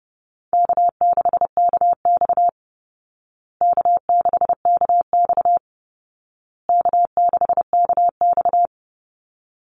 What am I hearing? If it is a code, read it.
Morse code, heard as a keyed tone, about 30 wpm, 700 Hz. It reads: K6KX K6KX K6KX
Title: Call Signs Repeated 2x - 30wpm